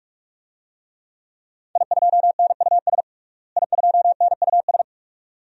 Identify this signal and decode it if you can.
Morse code, heard as a keyed tone, about 42 wpm, 700 Hz. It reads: I2NUS I2NUS
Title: DXCC-100 Call Signs - 5 Characters - Repeated 2x - 45wpm